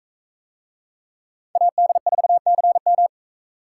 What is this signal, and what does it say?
Morse code, heard as a keyed tone, about 41 wpm, 700 Hz. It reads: AD4CM